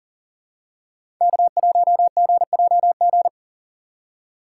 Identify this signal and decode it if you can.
Morse code, heard as a keyed tone, about 40 wpm, 700 Hz. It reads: K1GJG
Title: USA Call Signs - 5 Characters - 4x Character Spacing - 40wpm